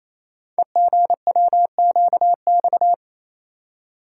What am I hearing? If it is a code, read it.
Morse code, heard as a keyed tone, about 29 wpm, 700 Hz. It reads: EGWQX
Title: Code Groups - 5 Letters - 28wpm